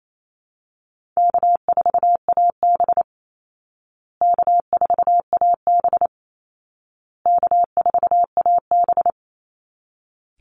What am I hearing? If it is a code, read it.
Morse code, heard as a keyed tone, about 28 wpm, 700 Hz. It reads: K4AB K4AB K4AB
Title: Call Signs Repeated 2x - 28wpm